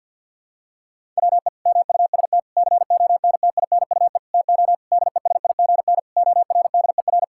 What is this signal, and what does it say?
Morse code, heard as a keyed tone, about 50 wpm, 700 Hz. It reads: WE MUST CONTINUE TO DESIGN ORDER
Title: Sentences from Top 1000 Words - Review Part 6 - 50wpm